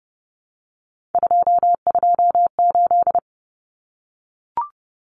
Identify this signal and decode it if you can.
Morse code, heard as a keyed tone, about 30 wpm, 700 Hz. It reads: 228
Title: Common Cut Numbers - 3-Digit 30wpm